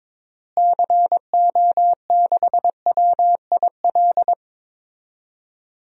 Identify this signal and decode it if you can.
Morse code, heard as a keyed tone, about 22 wpm, 700 Hz. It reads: CO6WIL
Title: DXCC-100 Call Signs - 6 Characters - Repeated 2x - 22wpm